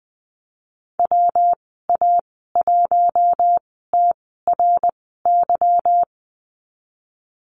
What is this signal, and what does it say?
Morse code, heard as a keyed tone, about 19 wpm, spaced as for 14 wpm, 700 Hz. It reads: WA1TRY
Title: 147 Lesson 24g - Letter y - Call Signs 20wpm